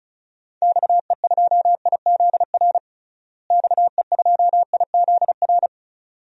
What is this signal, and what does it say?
Morse code, heard as a keyed tone, about 34 wpm, 700 Hz. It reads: XE2IZR XE2IZR
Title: DXCC-100 Call Signs - 6 Characters - Repeated 2x - 35wpm